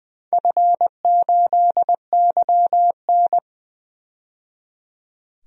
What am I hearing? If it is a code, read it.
Morse code, heard as a keyed tone, about 20 wpm, 700 Hz. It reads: F8YN
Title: DXCC-100 Call Signs - 4 Characters - 20wpm